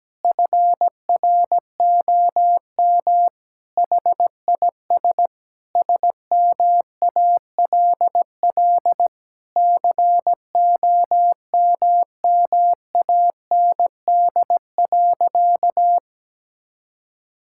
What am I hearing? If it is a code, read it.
Morse code, heard as a keyed tone, about 17 wpm, 700 Hz. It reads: FROM HIS SMALL COMMAND.
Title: Sentences from Top 1600 Words 17wpm